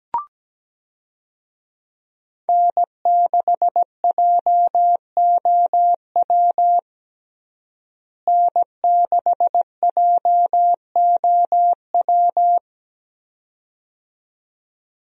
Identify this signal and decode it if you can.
Morse code, heard as a keyed tone, about 17 wpm, 700 Hz. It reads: N6JOW N6JOW
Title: USA Call Signs - 5 Characters - Repeated 2x - 17wpm